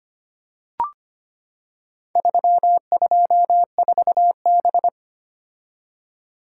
Morse code, heard as a keyed tone, about 25 wpm, 700 Hz. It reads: 324B
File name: practice / 25wpm / Four Letter-Number Combinations - 25wpm